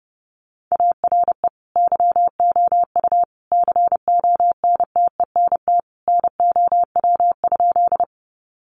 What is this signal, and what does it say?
Morse code, heard as a keyed tone, about 30 wpm, 700 Hz. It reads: ARE YOU CONTENT NOW?
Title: Sentences from Top 900 Words 30wpm